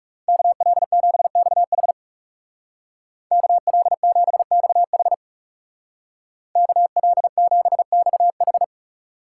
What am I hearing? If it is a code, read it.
Morse code, heard as a keyed tone, about 36 wpm, 700 Hz. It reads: KL7XH KL7XH KL7XH
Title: Speed Racing - USA 2x2 Call Signs 30wpm